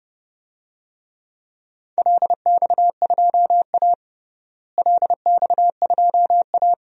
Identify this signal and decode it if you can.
Morse code, heard as a keyed tone, about 30 wpm, 700 Hz. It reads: LX2A LX2A
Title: DXCC-100 Call Signs - 4 Characters - Repeated 2x - 30wpm